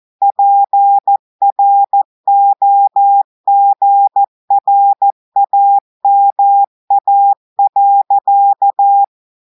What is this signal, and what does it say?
Morse code, heard as a keyed tone, about 14 wpm, 800 Hz. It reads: PROGRAMA.
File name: audios / portugais / portugais texte en clair 2 vitesse 14 mots minutes